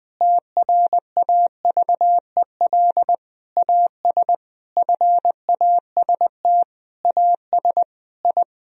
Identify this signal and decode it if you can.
Morse code, heard as a keyed tone, about 20 wpm, 700 Hz. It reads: TRAVEL AS FAST AS I
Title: Sentences from Top 1000 Words - Review Part 4 - 20wpm